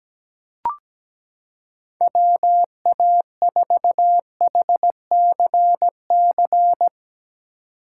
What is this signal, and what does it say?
Morse code, heard as a keyed tone, about 17 wpm, 700 Hz. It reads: WA4HCC
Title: K1USN SST QSO Practice - 17wpm